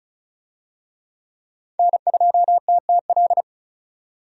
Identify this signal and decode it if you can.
Morse code, heard as a keyed tone, about 34 wpm, 700 Hz. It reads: N2TTL
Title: USA Call Signs - 5 Characters - 35wpm